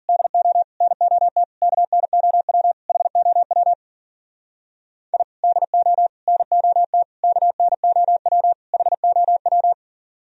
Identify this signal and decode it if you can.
Morse code, heard as a keyed tone, about 48 wpm, 700 Hz. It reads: DO NOT KNOW HOW I DO NOT KNOW HOW
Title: Speed-Racing - Sets of 5 Words 40wpm